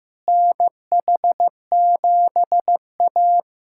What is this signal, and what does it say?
Morse code, heard as a keyed tone, about 15 wpm, 700 Hz. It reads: NH7A
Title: USA Callsign DE Callsign - 4 Characters - 15wpm